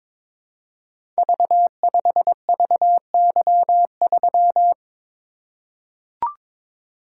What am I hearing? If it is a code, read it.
Morse code, heard as a keyed tone, about 22 wpm, 700 Hz. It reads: V5VY3
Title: Code Groups - 5 Letter:Numbers - 22wpm